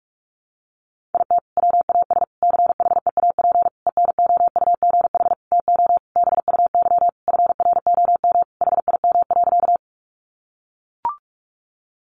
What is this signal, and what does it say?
Morse code, heard as a keyed tone, about 44 wpm, 700 Hz. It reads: IT WAS CHEAP ENOUGH TO BUY FROM HIM.
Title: Sentences from Top 1200 Words 45wpm